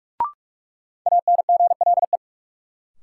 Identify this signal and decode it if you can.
Morse code, heard as a keyed tone, about 46 wpm, 700 Hz. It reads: ANGLE
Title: Sentences from Top 1500 Words - New Word Drill - 45wpm